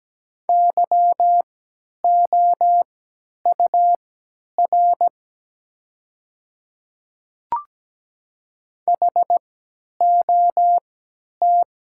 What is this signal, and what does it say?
Morse code, heard as a keyed tone, about 17 wpm, spaced as for 10 wpm, 700 Hz. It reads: YOUR HOT
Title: 3x Character Spacing - Top 100 Words - 17wpm